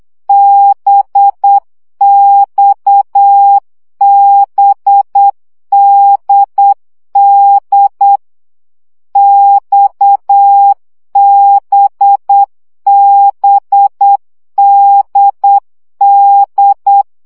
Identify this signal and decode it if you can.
Morse code, heard as a keyed tone, about 8 wpm, 800 Hz. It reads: BXBDD XBBDD
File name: CW gyakorlas d b x 7 baud